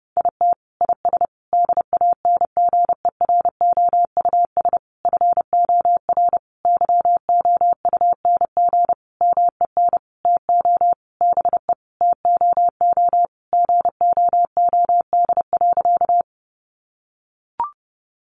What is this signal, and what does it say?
Morse code, heard as a keyed tone, about 30 wpm, 700 Hz. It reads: IT IS DANGEROUS FOR YOUNG MEN TO BE TOO GOOD.
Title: Sentences from Top 1200 Words 30wpm